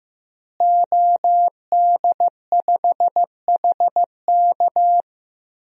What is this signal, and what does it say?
Morse code, heard as a keyed tone, about 16 wpm, 700 Hz. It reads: OD5HK
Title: DXCC-100 Call Signs - 5 Characters - Repeated 3x - 15wpm